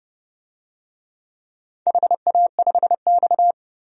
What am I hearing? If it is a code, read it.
Morse code, heard as a keyed tone, about 28 wpm, 700 Hz. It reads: HA5X
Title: DXCC-100 Call Signs - 4 Characters - 30wpm